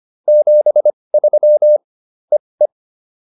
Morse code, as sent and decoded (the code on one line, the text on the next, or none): --... ...-- / . .
73 EE